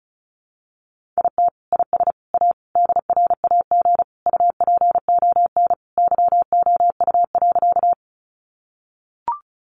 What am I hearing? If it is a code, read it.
Morse code, heard as a keyed tone, about 34 wpm, 700 Hz. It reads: IT IS A DRAG UPON YOU.
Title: Sentences from Top 1500 Words 35wpm